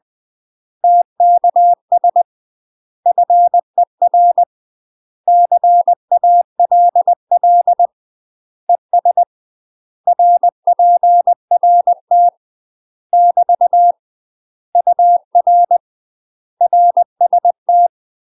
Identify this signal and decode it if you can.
Morse code, heard as a keyed tone, about 20 wpm, 700 Hz. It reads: TKS FER CALL ES RPRT <BT> UR RST
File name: UK MASTER QSOs TRAINING / 20WPM / UK MASTER QSOs 0014